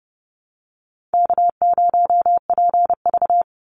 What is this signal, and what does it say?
Morse code, heard as a keyed tone, about 30 wpm, 700 Hz. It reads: K0PV
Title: Call Signs - 30wpm